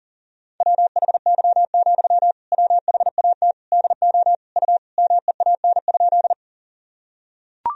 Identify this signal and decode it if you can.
Morse code, heard as a keyed tone, about 40 wpm, 700 Hz. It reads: WHY, WHAT DO U MEAN?
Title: Sentences from Top 200 Words and Abbreviations 40wpm